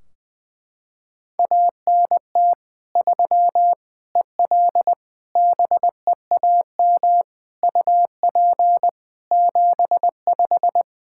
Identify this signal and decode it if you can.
Morse code, heard as a keyed tone, about 21 wpm, 700 Hz. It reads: ANT 3 EL BEAM UP 75
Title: Practice QSO - Exchange 2 - Info - 20wpm